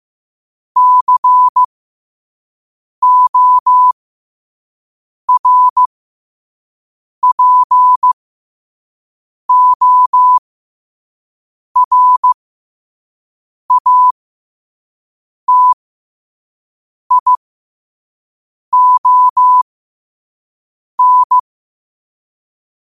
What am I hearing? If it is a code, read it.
Morse code, heard as a keyed tone, about 15 wpm, spaced as for 5 wpm, 1000 Hz. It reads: CORPORATION
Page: Quotes for Fri, 15 Aug 2025 in Morse Code at 5 words per minute.